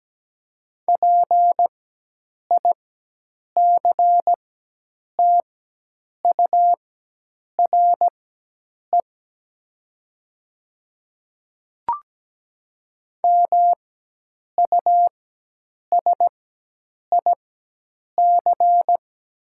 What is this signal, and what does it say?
Morse code, heard as a keyed tone, about 17 wpm, spaced as for 8 wpm, 700 Hz. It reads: PICTURE MUSIC
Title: 4x Character Spacing - Top 300 Words - 17wpm